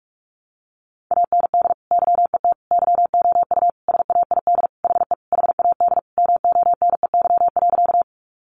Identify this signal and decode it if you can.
Morse code, heard as a keyed tone, about 44 wpm, 700 Hz. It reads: AND YET YOU SAID HE HAD MONEY.
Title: Sentences from Top 500 Words 45wpm